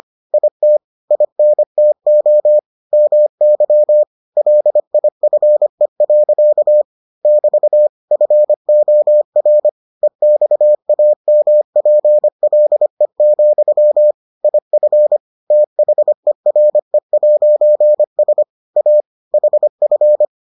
Morse code, heard as a keyed tone, about 25 wpm, 600 Hz. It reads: IT INTO MY LIFE. <BT> FOR EXAMPLE, IF THERE'S A HF